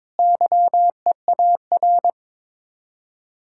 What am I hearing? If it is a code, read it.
Morse code, heard as a keyed tone, about 22 wpm, 700 Hz. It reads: YEAR
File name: practice / 22wpm / Sets of 7 Words - 22wpm